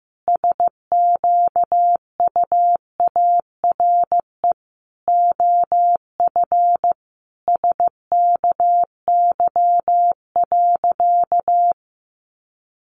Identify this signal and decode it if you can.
Morse code, heard as a keyed tone, about 15 wpm, 700 Hz. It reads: SQUARE OF SKY.
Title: Sentences from Top 1200 Words 15wpm